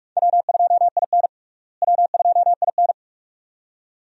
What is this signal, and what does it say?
Morse code, heard as a keyed tone, about 42 wpm, 700 Hz. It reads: W2IN W2IN
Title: USA Call Signs - 4 Characters - Repeated 2x - 45wpm